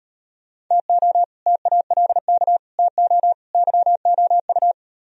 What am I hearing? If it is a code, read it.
Morse code, heard as a keyed tone, about 38 wpm, 700 Hz. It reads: TO TALK TO YOU